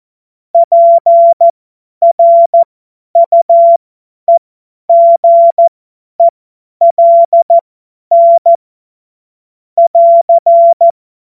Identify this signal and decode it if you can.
Morse code, heard as a keyed tone, about 14 wpm, spaced as for 10 wpm, 680 Hz. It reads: PRUEGELN <AR>
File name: Morsekurs / Witze / Witz 29. Juni 2021, 50 F70